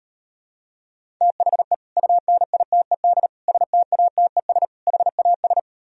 Morse code, heard as a keyed tone, about 38 wpm, 700 Hz. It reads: THE UNITED STATES HAS